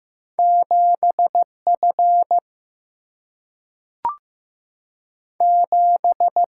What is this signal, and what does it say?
Morse code, heard as a keyed tone, about 15 wpm, 700 Hz. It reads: 7F 7
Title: Single-Letter Single-Number 15wpm